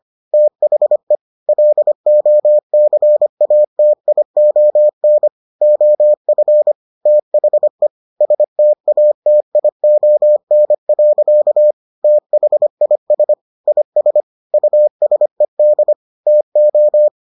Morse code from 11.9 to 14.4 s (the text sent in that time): THIS IS